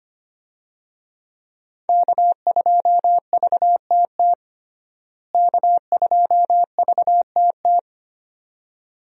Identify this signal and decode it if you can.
Morse code, heard as a keyed tone, about 25 wpm, 700 Hz. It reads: K2VTT K2VTT
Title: USA Call Signs - 5 Characters - Repeated 2x - 25wpm